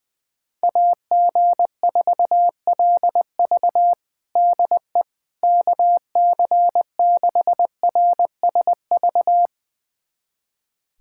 Morse code, heard as a keyed tone, about 21 wpm, 700 Hz. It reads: AG4LV DE KC6RSV
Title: USA Callsign DE Callsign 20wpm